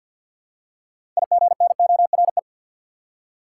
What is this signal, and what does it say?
Morse code, heard as a keyed tone, about 53 wpm, 700 Hz. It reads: IGNORE
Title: Six Letter Words - 50wpm